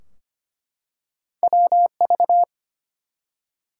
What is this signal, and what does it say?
Morse code, heard as a keyed tone, about 25 wpm, 700 Hz. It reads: WV
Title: US States and Canadian Provinces 25wpm